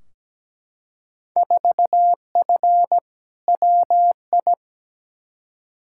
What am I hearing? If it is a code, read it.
Morse code, heard as a keyed tone, about 17 wpm, 700 Hz. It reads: 4F WI
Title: Field Day Exchanges - 17wpm